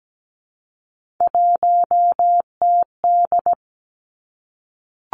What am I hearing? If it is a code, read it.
Morse code, heard as a keyed tone, about 17 wpm, 700 Hz. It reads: 1TD
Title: Three Challenging Letter-Number Combinations 17wpm